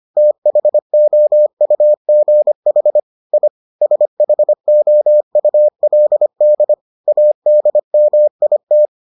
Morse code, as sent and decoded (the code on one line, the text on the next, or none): - .... --- ..- --. .... / .. / ... .... --- ..- .-.. -.. / .- -.. -- .. -
THOUGH I SHOULD ADMIT